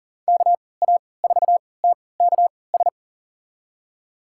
Morse code, heard as a keyed tone, about 40 wpm, spaced as for 23 wpm, 700 Hz. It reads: KA4TKS
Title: USA Call Signs - 3x Character Spacing - 40wpm